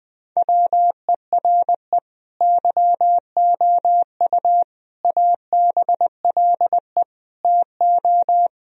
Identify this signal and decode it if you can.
Morse code, heard as a keyed tone, about 21 wpm, 700 Hz. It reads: WERE YOU ABLE TO
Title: Sentences from Top 1000 Words - Review Part 4 - 20wpm